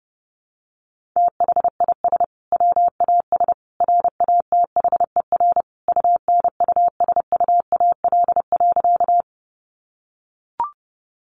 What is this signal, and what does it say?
Morse code, heard as a keyed tone, about 30 wpm, 700 Hz. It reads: THIS WAS RATHER UNUSUAL.
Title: Sentences from Top 1400 Words 30wpm